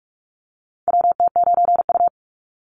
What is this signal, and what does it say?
Morse code, heard as a keyed tone, about 44 wpm, 700 Hz. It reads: WT9U
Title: Call Signs Repeated 3x - 45wpm